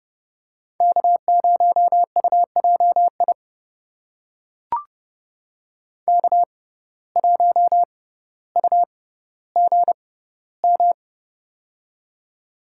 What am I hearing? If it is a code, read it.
Morse code, heard as a keyed tone, about 30 wpm, spaced as for 15 wpm, 700 Hz. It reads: K0UJI K1UGM
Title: USA Call Signs - 5 Characters - 6x Character Spacing - 30wpm